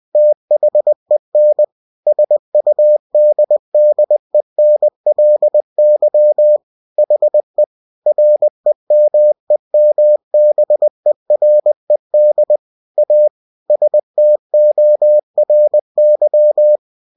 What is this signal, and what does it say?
Morse code, heard as a keyed tone, about 20 wpm, 600 Hz. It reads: THEN SUDDENLY HE REMEMBERED A STORY
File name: cw-books / winnie-the-pooh / winnie-20wpm-page-0092